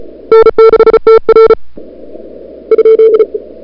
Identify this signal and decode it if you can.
Morse code, heard as a keyed tone, about 36 wpm, 440 Hz. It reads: N6TR ?
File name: CWSprint / February 2007 CW Sprint / 01-2007 CW Sprint Hour1